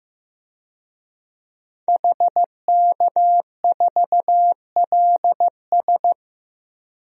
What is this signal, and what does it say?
Morse code, heard as a keyed tone, about 15 wpm, 700 Hz. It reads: HK4LS
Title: DXCC-100 Call Signs - 5 Characters - 15wpm